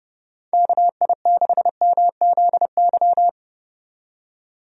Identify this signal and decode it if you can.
Morse code, heard as a keyed tone, about 29 wpm, 700 Hz. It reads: KI6MZY
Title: USA Call Signs - 6 Characters - 30wpm